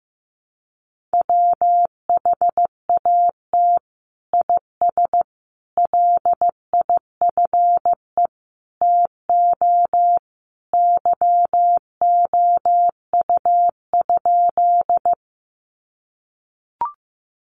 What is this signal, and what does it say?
Morse code, heard as a keyed tone, about 15 wpm, 700 Hz. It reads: WHAT IS LIFE TO YOU?
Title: Sentences from Top 300 Words 15wpm